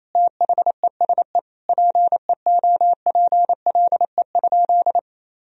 Morse code, heard as a keyed tone, about 28 wpm, 700 Hz. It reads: THESE PEOPLE?
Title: Sentences from Top 1000 Words - Review Part 3 - 28wpm